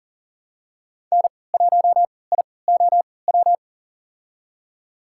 Morse code, heard as a keyed tone, about 40 wpm, spaced as for 23 wpm, 700 Hz. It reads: N1IOW
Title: USA Call Signs - 5 Characters - 3x Character Spacing - 40wpm